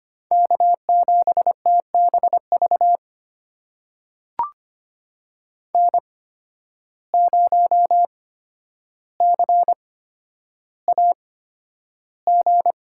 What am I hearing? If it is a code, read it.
Morse code, heard as a keyed tone, about 25 wpm, spaced as for 11 wpm, 700 Hz. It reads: K7TBV N0CAG
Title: USA Call Signs - 5 Characters - 8x Character Spacing - 25wpm